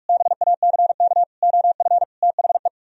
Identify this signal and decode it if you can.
Morse code, heard as a keyed tone, about 49 wpm, 700 Hz. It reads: BACK OF THE